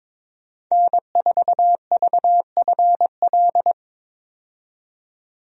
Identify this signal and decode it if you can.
Morse code, heard as a keyed tone, about 22 wpm, 700 Hz. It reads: N4VFL
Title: USA Call Signs - 5 Characters - 22wpm